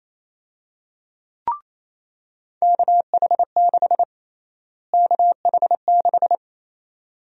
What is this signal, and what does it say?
Morse code, heard as a keyed tone, about 28 wpm, 700 Hz. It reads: KH6 KH6
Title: USA Call Sign Prefixes - Repeated 2x - 28wpm